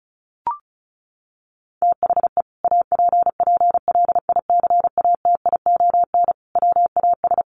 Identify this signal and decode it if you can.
Morse code, heard as a keyed tone, about 34 wpm, 700 Hz. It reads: THE APPLICATION WAS
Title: Sentences from Top 1200 Words 35wpm